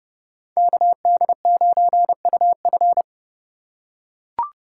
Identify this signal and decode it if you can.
Morse code, heard as a keyed tone, about 30 wpm, 700 Hz. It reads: KD9UF